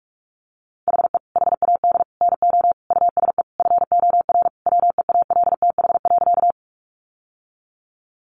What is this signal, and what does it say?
Morse code, heard as a keyed tone, about 44 wpm, 700 Hz. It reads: HE HAD NO USE FOR WEALTH.
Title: Sentences from Top 1200 Words 45wpm